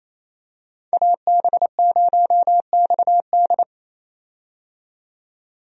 Morse code, heard as a keyed tone, about 27 wpm, 700 Hz. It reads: AB0XD
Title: USA Call Signs - 5 Characters - 28wpm